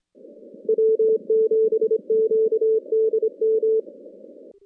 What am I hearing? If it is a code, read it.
Morse code, heard as a keyed tone, about 24 wpm, 410 Hz. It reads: W7QDM
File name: contests / CW-SS-k5tr- -2100